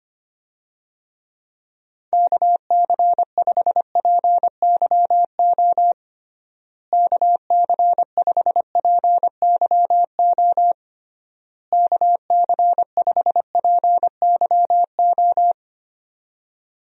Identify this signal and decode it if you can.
Morse code, heard as a keyed tone, about 25 wpm, 700 Hz. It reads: KC5PYO KC5PYO KC5PYO
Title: USA Call Signs - 6 Characters - Repeated 3x - 25wpm